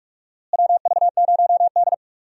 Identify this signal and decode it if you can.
Morse code, heard as a keyed tone, about 43 wpm, 700 Hz. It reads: WV0D
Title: USA Call Signs - 4 Characters - Repeated 3x - 45wpm